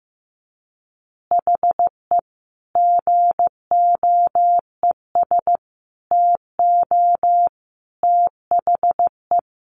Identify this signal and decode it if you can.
Morse code, heard as a keyed tone, about 15 wpm, 700 Hz. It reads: HE GOES TO THE
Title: Additional Easy Phrases 15wpm